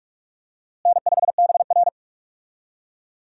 Morse code, heard as a keyed tone, about 46 wpm, 700 Hz. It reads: N5BR
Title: USA Call Signs - 4 Characters - Repeated 3x - 45wpm